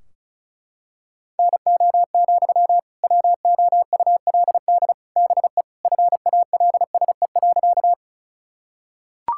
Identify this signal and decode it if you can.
Morse code, heard as a keyed tone, about 36 wpm, 700 Hz. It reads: NO, WOULD BE FALSE.
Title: Sentences from Top 1600 Words 35wpm